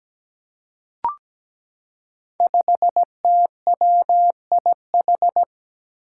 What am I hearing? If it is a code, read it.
Morse code, heard as a keyed tone, about 17 wpm, 700 Hz. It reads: 5TWIH
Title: Five Challenging Letter-Number Combinations 17wpm